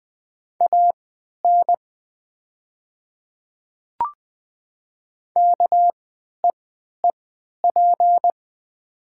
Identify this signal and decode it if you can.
Morse code, heard as a keyed tone, about 20 wpm, spaced as for 11 wpm, 700 Hz. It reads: AN KEEP
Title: 3x Character Spacing - Top 300 Words - 20wpm